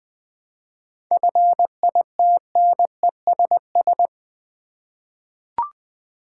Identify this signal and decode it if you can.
Morse code, heard as a keyed tone, about 20 wpm, 700 Hz. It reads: FITNESS
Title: 7 Letter Words - First 3 Letters Given - 20wpm